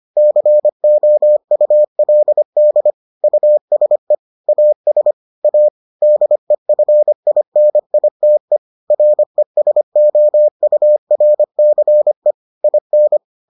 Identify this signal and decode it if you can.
Morse code, heard as a keyed tone, about 25 wpm, 600 Hz. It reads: COULD USE AS A DEFINITE RESOURCE IN